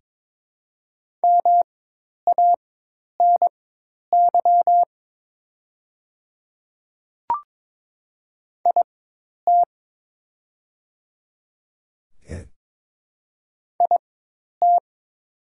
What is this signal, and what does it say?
Morse code, heard as a keyed tone, about 22 wpm, spaced as for 10 wpm, 700 Hz. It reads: MANY IT IT
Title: 4x Character Spacing - Top 200 Words - 22wpm